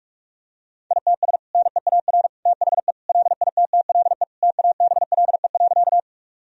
Morse code, heard as a keyed tone, about 46 wpm, 700 Hz. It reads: ITS NEAR THE LITTLE TABLE.
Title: Sentences from Top 1000 Words - Review Part 2 - 45wpm